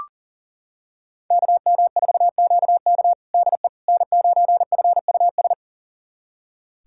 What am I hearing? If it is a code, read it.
Morse code, heard as a keyed tone, about 40 wpm, 700 Hz. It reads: KM4QK DE N9FUS
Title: USA Callsign DE Callsign - 5 Characters - 40wpm